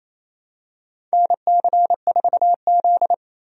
Morse code, heard as a keyed tone, about 28 wpm, 700 Hz. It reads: NC4Z
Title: USA Call Signs - 4 Characters - 28wpm